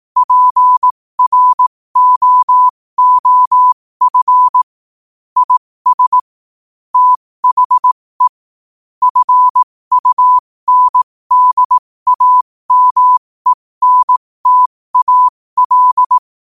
Morse code, heard as a keyed tone, about 18 wpm, spaced as for 15 wpm, 1000 Hz. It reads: PROOF IS THE FUNDAMENTAL
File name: qotdfiles / 15wpm